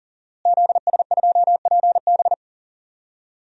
Morse code, heard as a keyed tone, about 40 wpm, 700 Hz. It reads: ZS2PB
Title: DXCC-100 Call Signs - 5 Characters - Repeated 2x - 40wpm